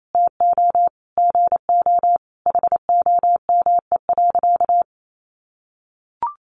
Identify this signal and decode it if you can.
Morse code, heard as a keyed tone, about 28 wpm, 700 Hz. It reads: TO GO HOME.
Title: Sentences from Top 1400 Words 28wpm